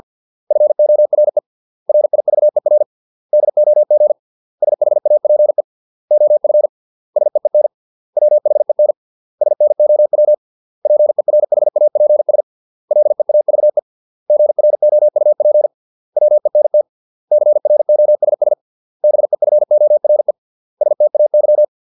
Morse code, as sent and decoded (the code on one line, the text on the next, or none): ..-. --- .-. / .-. .. ...- . .-. / -.. --- --. / ... .... .- .--. / --- ..-. / ... . -. / .-- .... -. / ... -. --- .-- / .--. .-. .... .- .--. ... / .-.. .- ...- . / --. .-. --- ..- .--. / .-- . -. - / -.-. .-. --- ... ... / -... ..-. --- .-. / ... - .- -.--
FORE RIVER DOG SHAPE OF SEEN WHEN SNOW PERHAPS LEAVE GROUP WENT CROSS BEFORE STAY